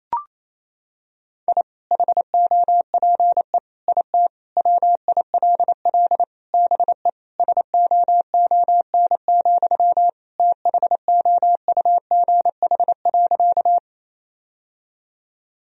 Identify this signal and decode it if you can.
Morse code, heard as a keyed tone, about 28 wpm, 700 Hz. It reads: I HOPE IT WILL BE SOON, THOUGH.
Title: Sentences from Top 1000 Words - Review Part 2 - 28wpm